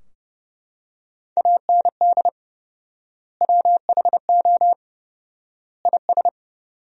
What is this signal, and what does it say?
Morse code, heard as a keyed tone, about 32 wpm, 700 Hz. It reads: AND WHO IS
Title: 4x Word-Spacing - Sentences from Top 200 Words - 30wpm